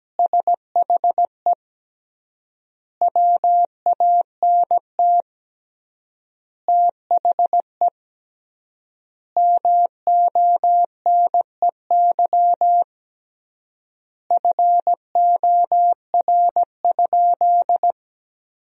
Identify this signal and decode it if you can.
Morse code, heard as a keyed tone, about 17 wpm, 700 Hz. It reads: SHE WANT THE MONEY FOR?
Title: 3x Word-Spacing - Sentences from Top 400 Words - 17wpm